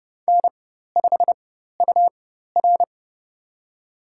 Morse code, heard as a keyed tone, about 30 wpm, spaced as for 14 wpm, 700 Hz. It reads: N5UR
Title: USA Call Signs - 4 Characters - 4x Character Spacing - 30wpm